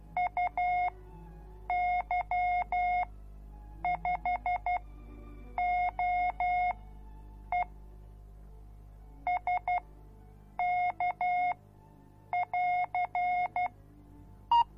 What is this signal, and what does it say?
Morse code, heard as a keyed tone, about 12 wpm, spaced as for 7 wpm, 720 Hz. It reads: UY5OE SK<AR>